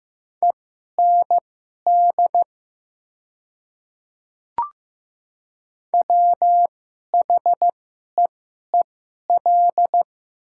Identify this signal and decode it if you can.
Morse code, heard as a keyed tone, about 15 wpm, spaced as for 11 wpm, 700 Hz. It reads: END WHEEL
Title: 2x Character Spacing - Top 500 Words - 15wpm